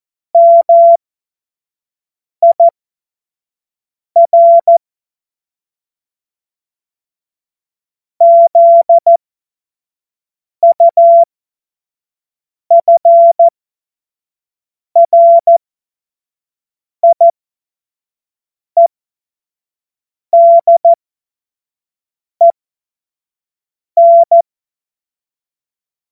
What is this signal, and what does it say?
Morse code, heard as a keyed tone, about 14 wpm, spaced as for 5 wpm, 680 Hz. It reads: MIR ZUFRIEDEN